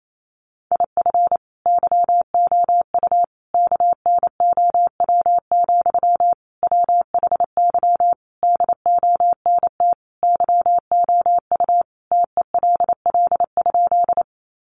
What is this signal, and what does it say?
Morse code, heard as a keyed tone, about 28 wpm, 700 Hz. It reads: IF YOU KNOW, WHY DONT YOU TELL?
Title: Sentences from Top 300 Words 28wpm